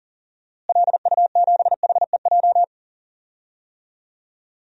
Morse code, heard as a keyed tone, about 40 wpm, 700 Hz. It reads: LU7HEJ
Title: DXCC-100 Call Signs - 6 Characters - Repeated 3x - 40wpm